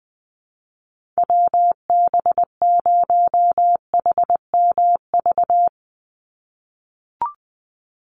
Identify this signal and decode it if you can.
Morse code, heard as a keyed tone, about 20 wpm, 700 Hz. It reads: WB0HMV
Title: USA Call Signs 20wpm